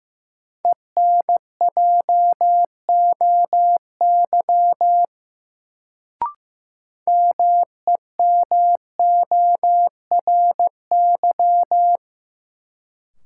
Morse code, heard as a keyed tone, about 15 wpm, 700 Hz. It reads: ENJOY MEMORY
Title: Sentences from Top 1300 Words - New Word Drill Alt - 15wpm